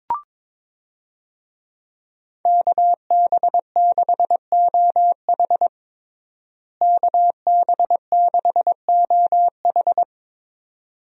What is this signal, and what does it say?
Morse code, heard as a keyed tone, about 22 wpm, 700 Hz. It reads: KB6OH KB6OH
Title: USA Call Signs - 5 Characters - Repeated 2x - 22wpm